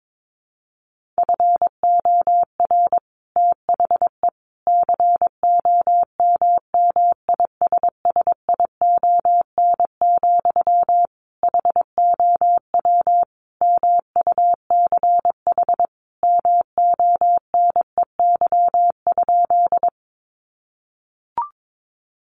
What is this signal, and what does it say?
Morse code, heard as a keyed tone, about 22 wpm, 700 Hz. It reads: FOR THE COMMISSION, HOW MUCH MONEY?
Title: Sentences from Top 1400 Words 22wpm